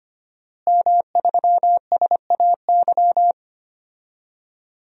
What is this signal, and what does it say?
Morse code, heard as a keyed tone, about 25 wpm, 700 Hz. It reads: M3SAY
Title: DXCC-100 Call Signs - 5 Characters - Repeated 3x - 25wpm